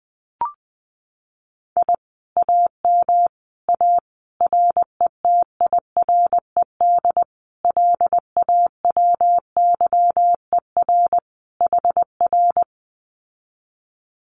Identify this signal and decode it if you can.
Morse code, heard as a keyed tone, about 20 wpm, 700 Hz. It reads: I AM A RETIRED LAWYER HR
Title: Common QSO Phrases Encore 20wpm